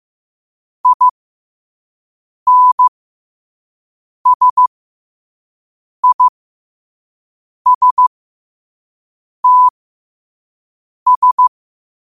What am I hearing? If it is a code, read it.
Morse code, heard as a keyed tone, about 15 wpm, spaced as for 5 wpm, 1000 Hz. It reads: INSISTS
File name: qotdfiles / 5wpm